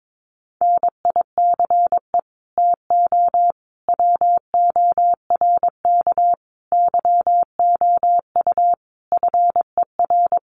Morse code, heard as a keyed tone, about 22 wpm, 700 Hz. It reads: NICE TO WORK YOU FER
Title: Common QSO Phrases 22wpm